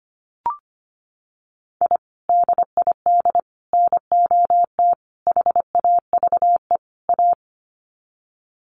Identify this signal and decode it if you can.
Morse code, heard as a keyed tone, about 25 wpm, 700 Hz. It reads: I DID NOT HAVE A
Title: Sets of 5 Words 25wpm